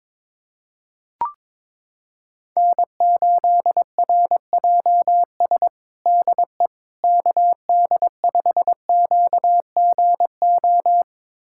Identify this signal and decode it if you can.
Morse code, heard as a keyed tone, about 22 wpm, 700 Hz. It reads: N8RJS DE KD5QGO
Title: USA Callsign DE Callsign 22wpm